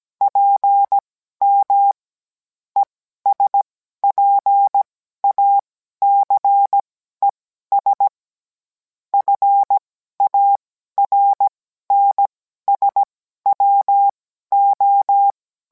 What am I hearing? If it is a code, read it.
Morse code, heard as a keyed tone, about 17 wpm, spaced as for 12 wpm, 800 Hz. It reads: PM ESPACES FARNSWO